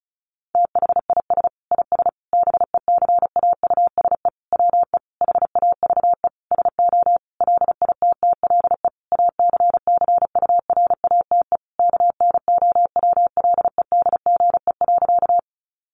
Morse code, heard as a keyed tone, about 34 wpm, 700 Hz. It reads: THIS IS BECAUSE WE HAVE SO LITTLE ACCURATE KNOWLEDGE.
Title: Sentences from Top 1200 Words 35wpm